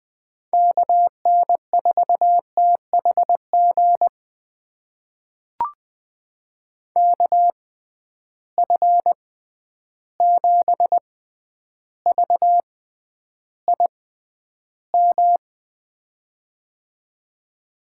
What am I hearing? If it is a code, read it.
Morse code, heard as a keyed tone, about 20 wpm, spaced as for 10 wpm, 700 Hz. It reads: KN4THG KF7VIM
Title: USA Call Signs - 6x Character Spacing - 20wpm